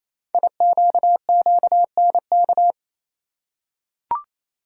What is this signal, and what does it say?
Morse code, heard as a keyed tone, about 27 wpm, 700 Hz. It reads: IQQNK
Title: Code Groups - 5 Letters - 28wpm